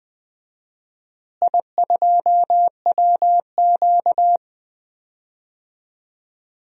Morse code, heard as a keyed tone, about 20 wpm, 700 Hz. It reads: I2WQ
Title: DXCC-100 Call Signs - 4 Characters - Repeated 3x - 20wpm